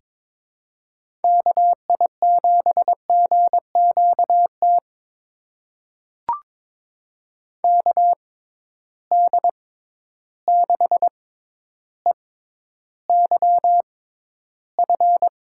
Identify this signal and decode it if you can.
Morse code, heard as a keyed tone, about 22 wpm, spaced as for 11 wpm, 700 Hz. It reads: KI7GQT KD6EYF
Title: USA Call Signs - 6 Characters - 6x Character Spacing - 22wpm